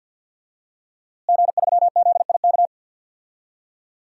Morse code, heard as a keyed tone, about 50 wpm, 700 Hz. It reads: G3ZIK
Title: DXCC-100 Call Signs - 5 Characters - Repeated 2x - 50wpm